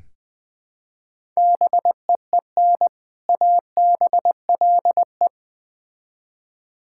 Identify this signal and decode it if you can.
Morse code, heard as a keyed tone, about 20 wpm, 700 Hz. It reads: BEEN ABLE
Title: Build A Phrase - 3-Word Phrases - 20wpm